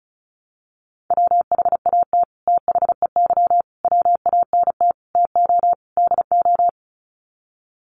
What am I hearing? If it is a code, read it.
Morse code, heard as a keyed tone, about 34 wpm, 700 Hz. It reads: WHAT THEY WANT TO DO
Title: Sets of 5 Words Encore2 35wpm